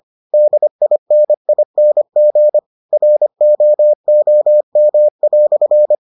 Morse code, heard as a keyed tone, about 25 wpm, 600 Hz. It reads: DINING ROOM"